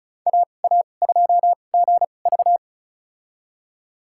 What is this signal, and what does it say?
Morse code, heard as a keyed tone, about 34 wpm, spaced as for 25 wpm, 700 Hz. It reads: AA2GV
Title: USA Call Signs - 5 Characters - 2x Character Spacing - 35wpm